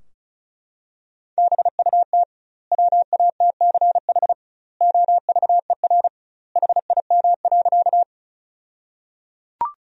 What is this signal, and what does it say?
Morse code, heard as a keyed tone, about 36 wpm, 700 Hz. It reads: BUT WATCH OVER HIM.
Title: 2x Word-Spacing - Sentences from Top 300 Words - 35wpm